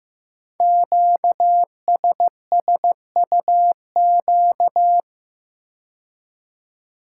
Morse code, heard as a keyed tone, about 15 wpm, 700 Hz. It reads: QSSUQ
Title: Code Groups - 5 Letters - 15wpm